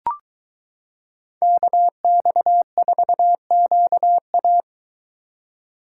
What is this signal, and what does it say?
Morse code, heard as a keyed tone, about 22 wpm, 700 Hz. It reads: KX4QA